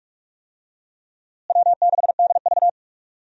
Morse code, heard as a keyed tone, about 46 wpm, 700 Hz. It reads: W6DV
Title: USA Call Signs - 4 Characters - Repeated 2x - 45wpm